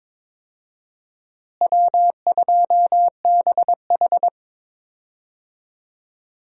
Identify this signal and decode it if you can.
Morse code, heard as a keyed tone, about 22 wpm, 700 Hz. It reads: W2BH
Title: USA Call Signs - 4 Characters - Repeated 3x - 22wpm